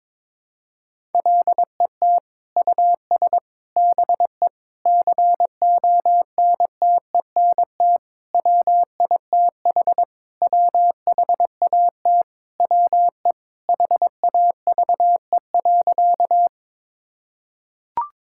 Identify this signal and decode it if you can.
Morse code, heard as a keyed tone, about 22 wpm, 700 Hz. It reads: LET US BE CONTENT WITH WHAT WE HAVE.
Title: Sentences from Top 1000 Words - Review Part 6 - 22wpm